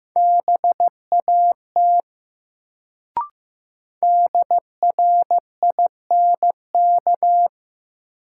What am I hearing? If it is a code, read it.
Morse code, heard as a keyed tone, about 15 wpm, 700 Hz. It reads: BAT DRINK
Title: Sentences from Top 1500 Words - New Word Drill Alt - 15wpm